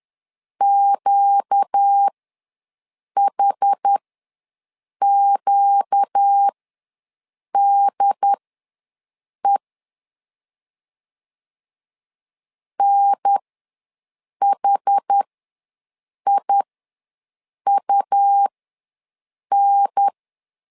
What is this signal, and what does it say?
Morse code, heard as a keyed tone, about 10 wpm, spaced as for 6 wpm, 800 Hz. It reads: QHQDE NHIUN
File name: LEZIONI / 4 Q ws 10wpm 2306